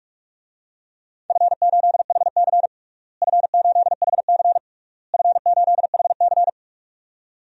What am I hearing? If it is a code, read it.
Morse code, heard as a keyed tone, about 43 wpm, 700 Hz. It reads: F8HC F8HC F8HC
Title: DXCC-100 Call Signs - 4 Characters - Repeated 3x - 45wpm